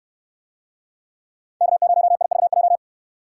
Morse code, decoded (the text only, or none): K0IVJ